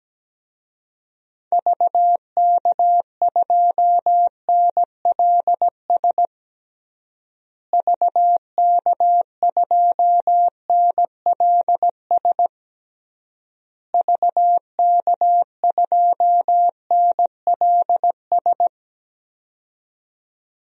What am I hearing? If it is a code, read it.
Morse code, heard as a keyed tone, about 17 wpm, 700 Hz. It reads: VK2NLS VK2NLS VK2NLS
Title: DXCC-100 Call Signs - 6 Characters - Repeated 3x - 17wpm